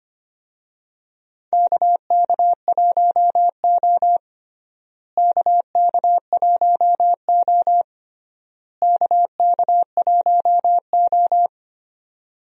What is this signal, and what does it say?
Morse code, heard as a keyed tone, about 25 wpm, 700 Hz. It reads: KK1O KK1O KK1O
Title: USA Call Signs - 4 Characters - Repeated 3x - 25wpm